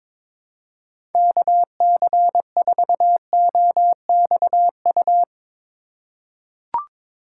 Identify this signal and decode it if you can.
Morse code, heard as a keyed tone, about 22 wpm, 700 Hz. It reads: KC4OXU
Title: USA Call Signs - 6 Characters - 4x Character Spacing - 22wpm